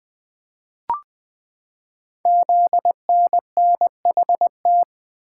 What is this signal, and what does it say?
Morse code, heard as a keyed tone, about 20 wpm, 700 Hz. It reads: ZNNHT
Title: Code Groups - 5 Letters - 20wpm